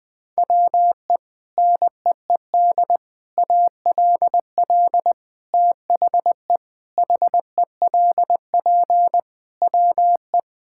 Morse code, heard as a keyed tone, about 20 wpm, 700 Hz. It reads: WE NEED ALL THE HELP WE
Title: Sets of 8 Words - 20wpm